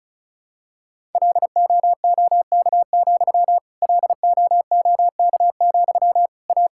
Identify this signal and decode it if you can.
Morse code, heard as a keyed tone, about 34 wpm, 700 Hz. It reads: LOOK, LOOK, A
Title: Sentences from Top 1000 Words - Review Part 5 - 35wpm